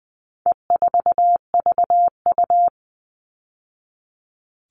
Morse code, heard as a keyed tone, about 20 wpm, 700 Hz. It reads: E4VU
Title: All Cut Numbers - 4-Digit 20wpm